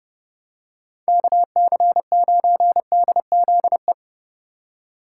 Morse code, heard as a keyed tone, about 29 wpm, 700 Hz. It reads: KC9DZE